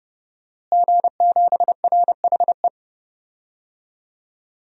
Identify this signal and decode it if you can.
Morse code, heard as a keyed tone, about 31 wpm, 700 Hz. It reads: G7RHE
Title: DXCC-100 Call Signs - 5 Characters - Repeated 3x - 30wpm